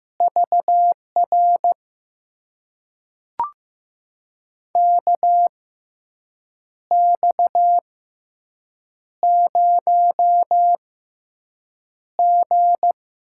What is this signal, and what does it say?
Morse code, heard as a keyed tone, about 15 wpm, spaced as for 6 wpm, 700 Hz. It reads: VR KX0G
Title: USA Call Signs - 4 Characters - 6x Character Spacing - 15wpm